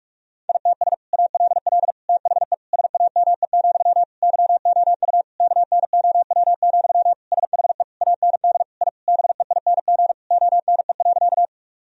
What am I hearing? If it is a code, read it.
Morse code, heard as a keyed tone, about 46 wpm, 700 Hz. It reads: ITS ALL THE SAME, YOU KNOW, SHE AND I BEING ONE.
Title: Sentences from Top 1000 Words - Review Part 3 - 45wpm